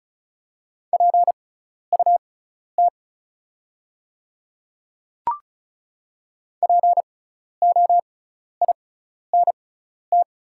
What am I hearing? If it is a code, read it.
Morse code, heard as a keyed tone, about 36 wpm, spaced as for 12 wpm, 700 Hz. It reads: PUT POINT
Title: 6x Character Spacing - Top 500 Words - 35wpm